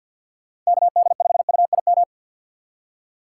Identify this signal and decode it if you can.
Morse code, heard as a keyed tone, about 52 wpm, 700 Hz. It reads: KD5UIM
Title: USA Call Signs - 2x Character Spacing - 50wpm